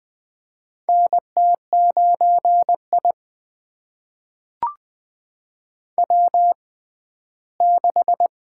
Morse code, heard as a keyed tone, about 20 wpm, spaced as for 14 wpm, 700 Hz. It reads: NT9I W6
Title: USA Call Signs - 6x Character Spacing - 20wpm